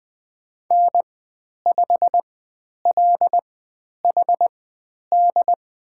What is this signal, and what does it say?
Morse code, heard as a keyed tone, about 20 wpm, spaced as for 10 wpm, 700 Hz. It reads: N5LHD
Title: 109 Lesson 18g - Letter m - Call Signs 20wpm